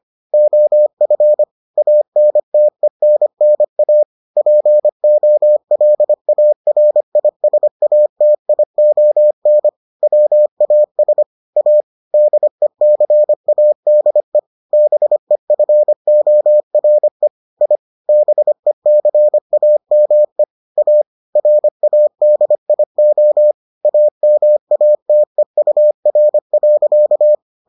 Morse code, heard as a keyed tone, about 25 wpm, 600 Hz. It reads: OF ANTENNA POLARISATION WAS A DECADE BEFORE I BECAME A RADIO AMATEUR.